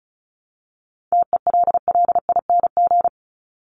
Morse code, heard as a keyed tone, about 34 wpm, 700 Hz. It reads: TELLING
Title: ING Suffix - Top 100 Words with Base Words 35wpm